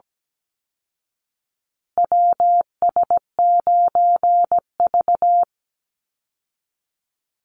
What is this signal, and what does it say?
Morse code, heard as a keyed tone, about 17 wpm, 700 Hz. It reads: WS9V
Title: Call Signs Repeated 3x - 17wpm